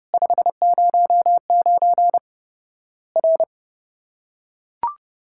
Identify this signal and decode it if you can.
Morse code, heard as a keyed tone, about 30 wpm, 700 Hz. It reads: 509 R
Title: Worked All Europe DX Contest - QTC Exchange - 30wpm